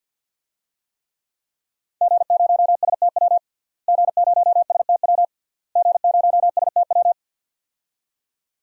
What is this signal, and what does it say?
Morse code, heard as a keyed tone, about 47 wpm, 700 Hz. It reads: G0STW G0STW G0STW
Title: DXCC-100 Call Signs - 5 Characters - Repeated 3x - 50wpm